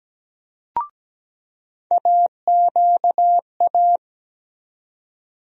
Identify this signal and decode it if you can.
Morse code, heard as a keyed tone, about 17 wpm, 700 Hz. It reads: AQA
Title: USA Call Sign Postfixes - 3 Letters 17wpm